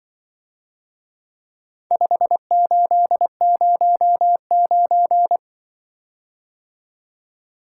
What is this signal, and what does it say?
Morse code, heard as a keyed tone, about 23 wpm, 700 Hz. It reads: 5809
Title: Speed-Racing - Four Letter-Number Combinations - 20wpm